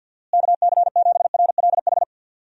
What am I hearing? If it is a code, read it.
Morse code, heard as a keyed tone, about 50 wpm, 700 Hz. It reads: KC7RLH
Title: USA Call Signs - 6 Characters - Repeated 3x - 50wpm